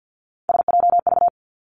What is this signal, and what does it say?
Morse code, heard as a keyed tone, about 50 wpm, 700 Hz. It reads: SJV
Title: ARRL Sections 50wpm